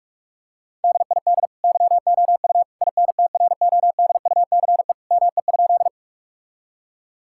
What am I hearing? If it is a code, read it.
Morse code, heard as a keyed tone, about 46 wpm, 700 Hz. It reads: DID YOU INTRODUCE ME?
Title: Sentences from Top 1600 Words 45wpm